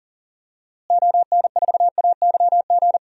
Morse code, heard as a keyed tone, about 40 wpm, 700 Hz. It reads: ON4AYG
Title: DXCC-100 Call Signs - 6 Characters - Repeated 2x - 40wpm